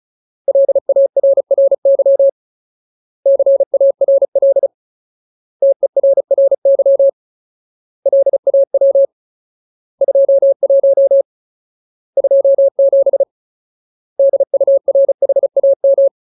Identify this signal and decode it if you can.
Morse code, heard as a keyed tone, about 34 wpm, 550 Hz. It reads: LARRY CARL TERRY LAW 21 27 DURHAM
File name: Ultimate Word List / 35WPM Ultimate